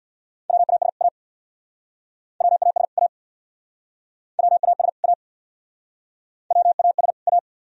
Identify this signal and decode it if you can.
Morse code, heard as a keyed tone, about 68 wpm, 700 Hz. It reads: WAS A WAS A WAS A WAS A